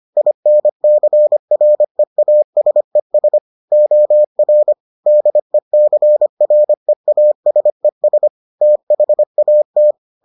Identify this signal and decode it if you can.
Morse code, heard as a keyed tone, about 25 wpm, 600 Hz. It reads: INCREASES OR DECREASES THAT